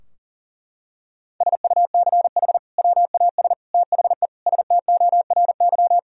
Morse code, heard as a keyed tone, about 40 wpm, 700 Hz. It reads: SUCH WAS THE STORY
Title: Sentences from Top 1000 Words - Review Part 3 - 40wpm